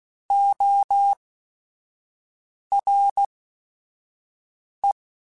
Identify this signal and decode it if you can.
Morse code, heard as a keyed tone, about 16 wpm, spaced as for 5 wpm, 800 Hz. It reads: ORE
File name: Technical Subjects / Technical Subjects-mp3s / XC @5